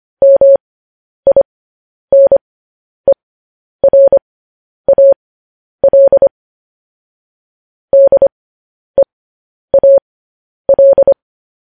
Morse code, heard as a keyed tone, about 25 wpm, spaced as for 10 wpm, 550 Hz. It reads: MINERAL DEAL
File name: podcast / News headlines-1x-25x10